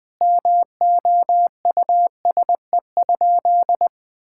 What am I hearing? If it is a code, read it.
Morse code, heard as a keyed tone, about 20 wpm, 700 Hz. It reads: MOUSE?
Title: Sentences from Top 1000 Words - Review Part 5 - 20wpm